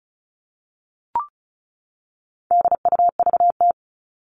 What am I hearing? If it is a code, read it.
Morse code, heard as a keyed tone, about 34 wpm, 700 Hz. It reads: DUVT
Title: All Cut Numbers - 4-Digit 35wpm